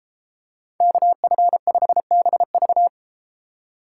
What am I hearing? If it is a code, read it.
Morse code, heard as a keyed tone, about 33 wpm, 700 Hz. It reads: KF5BV